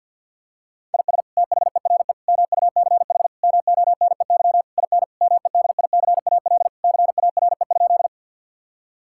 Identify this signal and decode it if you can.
Morse code, heard as a keyed tone, about 50 wpm, 700 Hz. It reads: IS THERE MUCH MONEY IN MEDICAL CARE?
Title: Sentences from Top 1000 Words - Review Part 5 - 50wpm